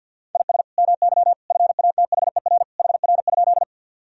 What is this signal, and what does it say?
Morse code, heard as a keyed tone, about 50 wpm, 700 Hz. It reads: IS MY FATHER HR?
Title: Sentences from Top 300 Words and Abbreviations 50wpm